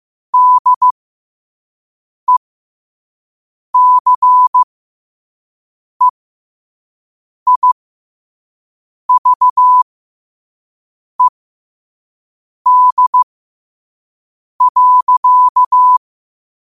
Morse code, heard as a keyed tone, about 15 wpm, spaced as for 5 wpm, 1000 Hz. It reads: DECEIVED.
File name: 5wpm